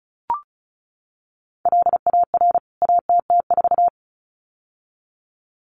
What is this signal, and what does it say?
Morse code, heard as a keyed tone, about 34 wpm, 700 Hz. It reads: LAR ATT4
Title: CWT Exchange 35wpm